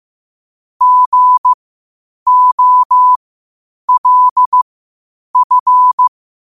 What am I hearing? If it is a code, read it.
Morse code, heard as a keyed tone, about 15 wpm, spaced as for 8 wpm, 1000 Hz. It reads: GOLF
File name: qotdfiles / 8wpm